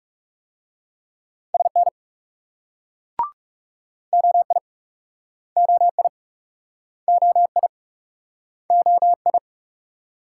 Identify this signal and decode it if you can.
Morse code, heard as a keyed tone, about 38 wpm, 700 Hz. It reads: SN OI OI OI OI